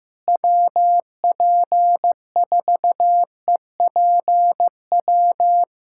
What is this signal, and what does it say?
Morse code, heard as a keyed tone, about 16 wpm, 700 Hz. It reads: WP4EPW
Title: USA Callsign DE Callsign - 6 Characters - 15wpm